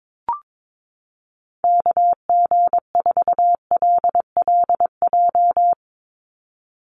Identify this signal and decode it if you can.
Morse code, heard as a keyed tone, about 22 wpm, 700 Hz. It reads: KG4LLJ
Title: USA Call Signs 22wpm